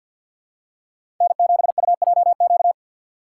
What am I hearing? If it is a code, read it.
Morse code, heard as a keyed tone, about 51 wpm, 700 Hz. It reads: N7UJQ
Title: USA Call Signs - 3x Character Spacing - 50wpm